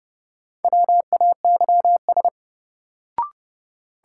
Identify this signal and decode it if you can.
Morse code, heard as a keyed tone, about 30 wpm, 700 Hz. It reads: WAYS
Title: Sets of 5 Words Encore - New Word Drill - 30wpm